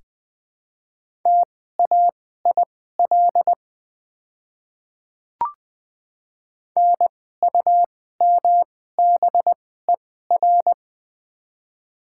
Morse code, heard as a keyed tone, about 20 wpm, spaced as for 14 wpm, 690 Hz. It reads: TAIL NUMBER
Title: 2x Character Spacing - Top 500 Words - 20wpm